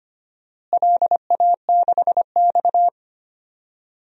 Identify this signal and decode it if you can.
Morse code, heard as a keyed tone, about 25 wpm, 700 Hz. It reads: LA6X
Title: DXCC-100 Call Signs - 4 Characters - Repeated 2x - 25wpm